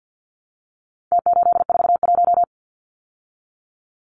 Morse code, heard as a keyed tone, about 49 wpm, 700 Hz. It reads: T841
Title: Numbers - Contest Style - 4-Digit 50wpm